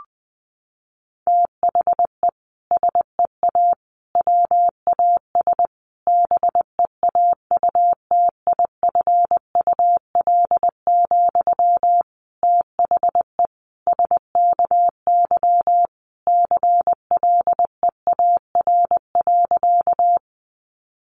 Morse code, heard as a keyed tone, about 20 wpm, 700 Hz. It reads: THE SEA WAS BEAUTIFUL, THE SKY CLEAR.
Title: Sentences from Top 1200 Words 20wpm